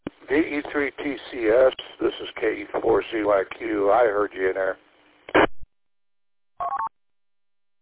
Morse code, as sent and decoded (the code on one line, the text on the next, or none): none